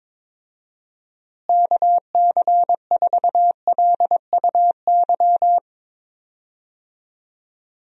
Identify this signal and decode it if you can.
Morse code, heard as a keyed tone, about 22 wpm, 700 Hz. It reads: KC4LUY
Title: USA Call Signs - 6 Characters - Repeated 3x - 22wpm